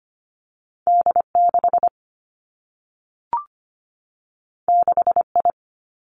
Morse code, heard as a keyed tone, about 25 wpm, 700 Hz. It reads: D 6 6 I
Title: Two Challenging Letter-Number Combinations 25wpm